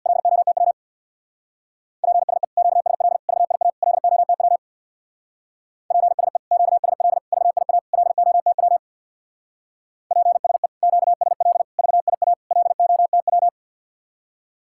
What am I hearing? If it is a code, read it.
Morse code, heard as a keyed tone, about 63 wpm, 700 Hz. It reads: LOTW PSE QSL VIA LOTW PSE QSL VIA LOTW PSE QSL VIA LOTW